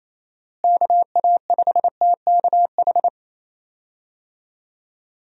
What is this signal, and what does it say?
Morse code, heard as a keyed tone, about 28 wpm, 700 Hz. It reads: KA5TKH
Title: USA Call Signs - 6 Characters - Repeated 2x - 28wpm